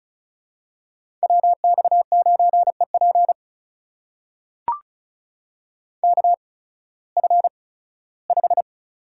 Morse code, heard as a keyed tone, about 36 wpm, spaced as for 19 wpm, 700 Hz. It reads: WX9EP KF5
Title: USA Call Signs - 8x Character Spacing - 35wpm